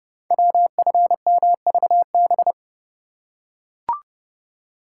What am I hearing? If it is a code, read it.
Morse code, heard as a keyed tone, about 29 wpm, 700 Hz. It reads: WFMVB
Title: Code Groups - 5 Letters - 30wpm